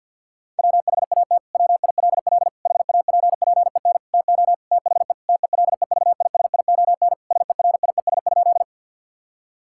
Morse code, heard as a keyed tone, about 46 wpm, 690 Hz. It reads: WHAT WILL HAPPEN TO THE TELEVISION SERIES?
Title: Sentences from Top 1000 Words - Review Part 5 - 50wpm